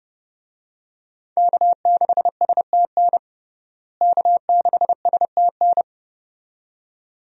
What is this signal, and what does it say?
Morse code, heard as a keyed tone, about 30 wpm, 700 Hz. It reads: K6STN K6STN
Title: USA Call Signs - 5 Characters - Repeated 2x - 30wpm